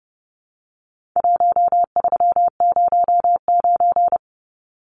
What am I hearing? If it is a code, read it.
Morse code, heard as a keyed tone, about 30 wpm, 700 Hz. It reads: 1309
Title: Numbers - 4-Digit 30wpm